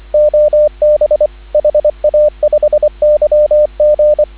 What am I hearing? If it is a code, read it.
Morse code, heard as a keyed tone, about 24 wpm, 590 Hz, with radio noise. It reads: OB HA5YG
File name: CW / 2017 HA-OB HA5YG